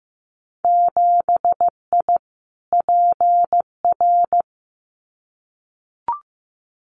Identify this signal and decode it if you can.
Morse code, heard as a keyed tone, about 15 wpm, 700 Hz. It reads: 7I PR
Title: Winter Field Day Exchanges Encore 15wpm